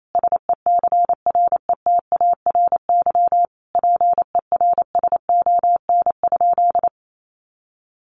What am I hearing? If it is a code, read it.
Morse code, heard as a keyed tone, about 28 wpm, 700 Hz. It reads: SECRETARY PERSON?
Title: Sentences from Top 1300 Words 28wpm